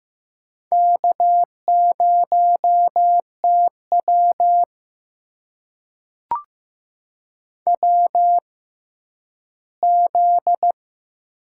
Morse code, heard as a keyed tone, about 15 wpm, spaced as for 10 wpm, 700 Hz. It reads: K0TW WZ
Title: USA Call Signs - 4 Characters - 6x Character Spacing - 15wpm